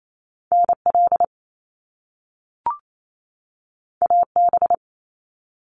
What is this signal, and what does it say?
Morse code, heard as a keyed tone, about 28 wpm, 700 Hz. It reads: NL AB
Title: Canadian Provinces and Territories 28wpm